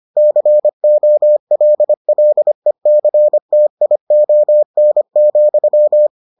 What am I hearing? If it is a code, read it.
Morse code, heard as a keyed tone, about 25 wpm, 600 Hz. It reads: COLLECTION,